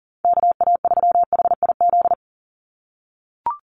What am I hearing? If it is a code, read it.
Morse code, heard as a keyed tone, about 40 wpm, 700 Hz. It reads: KA3HIZ
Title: USA Call Signs 40wpm